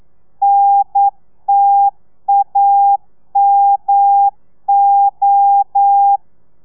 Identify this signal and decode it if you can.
Morse code, heard as a keyed tone, about 9 wpm, 800 Hz. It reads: NTAMO
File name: telegrafia / i2tfj / Lez2 40